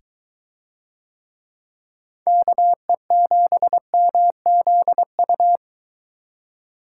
Morse code, heard as a keyed tone, about 23 wpm, 700 Hz. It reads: KE7MZU